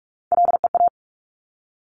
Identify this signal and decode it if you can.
Morse code, heard as a keyed tone, about 46 wpm, 690 Hz. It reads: LEA
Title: Top Three Letter Combinations in Words 45wpm